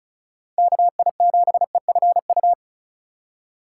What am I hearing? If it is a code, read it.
Morse code, heard as a keyed tone, about 36 wpm, 700 Hz. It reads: KI7EFU
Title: USA Call Signs - 2x Character Spacing - 35wpm